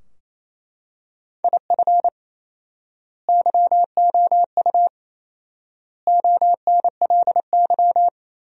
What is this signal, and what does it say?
Morse code, heard as a keyed tone, about 28 wpm, 700 Hz. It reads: IF YOU ONLY
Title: 4x Word-Spacing - Sentences from Top 400 Words - 28wpm